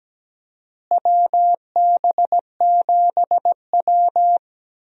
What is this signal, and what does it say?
Morse code, heard as a keyed tone, about 17 wpm, 700 Hz. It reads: WB7W
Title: USA Call Signs - 4 Characters - Repeated 3x - 17wpm